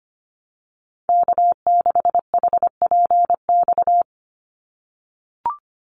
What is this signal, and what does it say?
Morse code, heard as a keyed tone, about 25 wpm, 690 Hz. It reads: K6HPX
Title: NA SOTA Activator Callsigns 25wpm